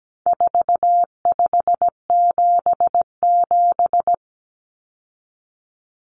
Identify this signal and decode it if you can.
Morse code, heard as a keyed tone, about 17 wpm, 700 Hz. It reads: 4577
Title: Common Cut Numbers - 4-Digit 17wpm